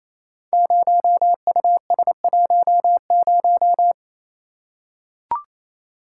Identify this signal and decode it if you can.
Morse code, heard as a keyed tone, about 28 wpm, 700 Hz. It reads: 0US10
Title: Code Groups - 5 Letter:Numbers - 28wpm